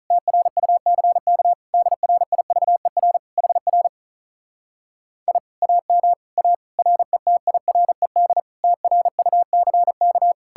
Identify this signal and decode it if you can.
Morse code, heard as a keyed tone, about 47 wpm, 700 Hz. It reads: TRUCK DRIVER HR I AM A RETIRED TRUCK